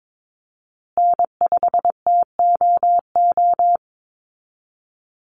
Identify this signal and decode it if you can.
Morse code, heard as a keyed tone, about 22 wpm, 700 Hz. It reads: N5TOO
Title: Call Signs Repeated 2x - 22wpm